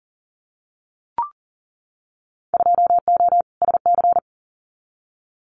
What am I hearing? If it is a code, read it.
Morse code, heard as a keyed tone, about 40 wpm, 700 Hz. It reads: 2O SC
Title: Winter Field Day Exchanges 40wpm